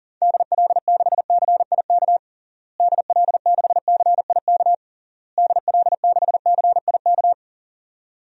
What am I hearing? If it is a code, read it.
Morse code, heard as a keyed tone, about 40 wpm, 700 Hz. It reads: DL6CIK DL6CIK DL6CIK
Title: DXCC-100 Call Signs - 6 Characters - Repeated 3x - 40wpm